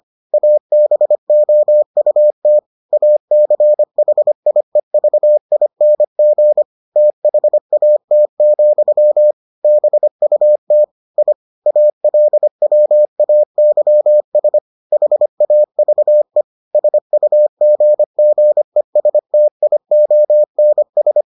Morse code, heard as a keyed tone, about 25 wpm, 600 Hz. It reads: ABOUT ACHIEVING THAT, BUT I ALWAYS HAVE SUGGESTIONS